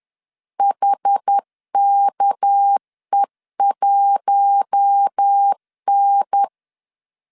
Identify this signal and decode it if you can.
Morse code, heard as a keyed tone, about 10 wpm, 800 Hz. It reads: HKE1N
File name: LEZIONI / 6 PKX0 / 6 PK 10wpm 2436